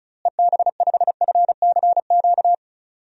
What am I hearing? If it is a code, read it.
Morse code, heard as a keyed tone, about 35 wpm, 700 Hz. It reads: EB5FCQ
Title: DXCC-100 Call Signs - 6 Characters - Repeated 2x - 35wpm